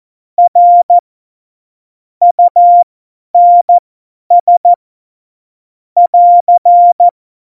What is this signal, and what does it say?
Morse code, heard as a keyed tone, about 14 wpm, spaced as for 10 wpm, 700 Hz. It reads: R UNS <AR>
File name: Morsekurs / Witze / Witz 07 Juli 2020, 50 F70